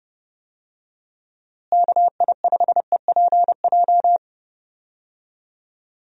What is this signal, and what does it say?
Morse code, heard as a keyed tone, about 30 wpm, 700 Hz. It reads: KI5EPJ
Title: USA Call Signs - 6 Characters - Repeated 2x - 30wpm